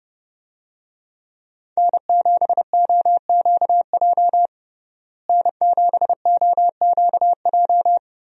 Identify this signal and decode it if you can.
Morse code, heard as a keyed tone, about 32 wpm, 700 Hz. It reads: N7OQJ N7OQJ
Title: USA Call Signs - 5 Characters - Repeated 3x - 30wpm